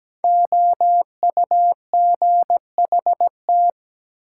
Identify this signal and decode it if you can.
Morse code, heard as a keyed tone, about 17 wpm, 700 Hz. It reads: OUGHT
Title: Five Letter Words - 17wpm